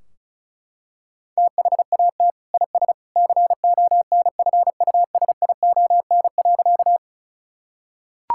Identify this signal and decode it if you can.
Morse code, heard as a keyed tone, about 34 wpm, 700 Hz. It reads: THAT IS CONFUSION.
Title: Sentences from Top 1600 Words 35wpm